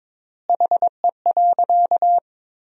Morse code, heard as a keyed tone, about 22 wpm, 700 Hz. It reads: HE.
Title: 4x Word-Spacing - Sentences from Top 100 Words - 22wpm